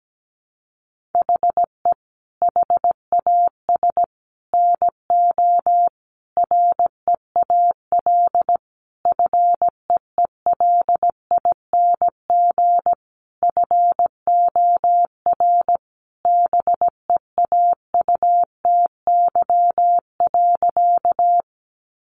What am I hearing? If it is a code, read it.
Morse code, heard as a keyed tone, about 17 wpm, 700 Hz. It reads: HE HAS NO REAL FEELING FOR BEAUTY.
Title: Sentences from Top 1200 Words 17wpm